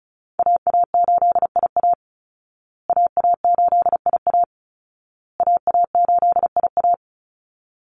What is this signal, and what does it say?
Morse code, heard as a keyed tone, about 34 wpm, 700 Hz. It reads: AA8IA AA8IA AA8IA
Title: Call Signs Repeated 2x - 35wpm